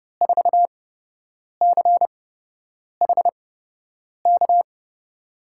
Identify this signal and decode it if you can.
Morse code, heard as a keyed tone, about 30 wpm, spaced as for 8 wpm, 700 Hz. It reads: 4CHK
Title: USA Call Signs - 8x Character Spacing - 30wpm